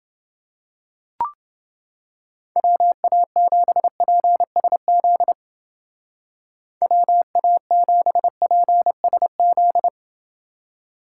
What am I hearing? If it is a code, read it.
Morse code, heard as a keyed tone, about 29 wpm, 700 Hz. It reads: WA7PSZ WA7PSZ